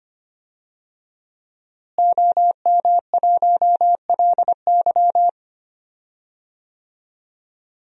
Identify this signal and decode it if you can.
Morse code, heard as a keyed tone, about 25 wpm, 700 Hz. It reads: OM1LY
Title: DXCC-100 Call Signs - 5 Characters - Repeated 3x - 25wpm